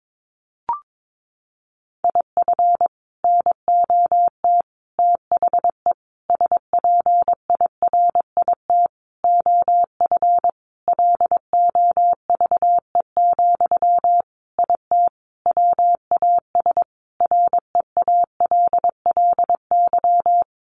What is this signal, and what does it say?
Morse code, heard as a keyed tone, about 22 wpm, 700 Hz. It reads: IF NOT THE SPIRIT OF LOVE, IT WAS REALLY
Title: Sentences from Top 1400 Words 22wpm